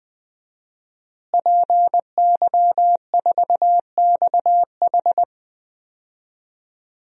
Morse code, heard as a keyed tone, about 20 wpm, 700 Hz. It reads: PY4XH
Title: DXCC-100 Call Signs - 5 Characters - Repeated 3x - 20wpm